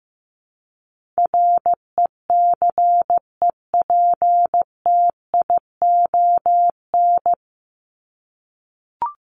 Morse code, heard as a keyed tone, about 15 wpm, 700 Hz. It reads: RECEPTION
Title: Top 1751-2000 Words 15wpm